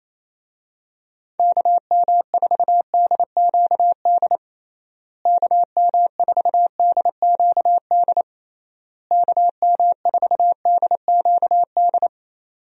Morse code, heard as a keyed tone, about 28 wpm, 700 Hz. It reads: KM4DQD KM4DQD KM4DQD
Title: USA Call Signs - 6 Characters - Repeated 3x - 28wpm